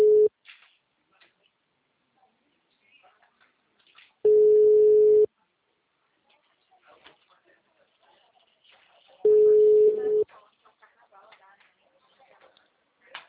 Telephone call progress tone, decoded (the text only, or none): ringback tone